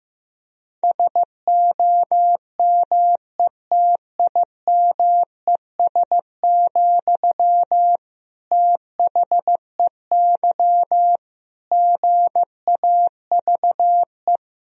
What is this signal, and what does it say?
Morse code, heard as a keyed tone, about 15 wpm, 700 Hz. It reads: SOMETIMES, THEY GAVE
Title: Sentences from Top 1000 Words - Review Part 4 - 15wpm